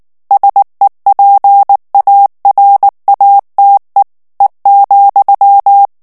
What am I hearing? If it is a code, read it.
Morse code, heard as a keyed tone, about 19 wpm, 800 Hz. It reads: SEPARATE E,
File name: morse mp3 / ITALIEN texte clair 2 vitesse 18 mots minute